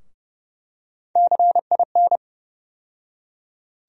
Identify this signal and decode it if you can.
Morse code, heard as a keyed tone, about 32 wpm, 700 Hz. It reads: CIN
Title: USA Call Sign Postfixes - 3 Letters 30wpm